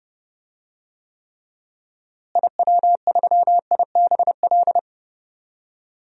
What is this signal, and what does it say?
Morse code, heard as a keyed tone, about 30 wpm, 700 Hz. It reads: IW3IBL
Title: DXCC-100 Call Signs - 6 Characters - Repeated 3x - 30wpm